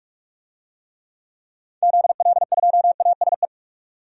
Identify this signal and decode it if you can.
Morse code, heard as a keyed tone, about 42 wpm, 700 Hz. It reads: ZL2ASE
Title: DXCC-100 Call Signs - 6 Characters - 45wpm